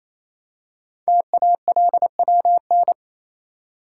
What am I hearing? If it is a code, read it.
Morse code, heard as a keyed tone, about 27 wpm, 700 Hz. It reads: TALWN
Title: Code Groups - 5 Letters - 28wpm